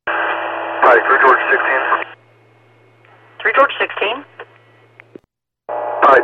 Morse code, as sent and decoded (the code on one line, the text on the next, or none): none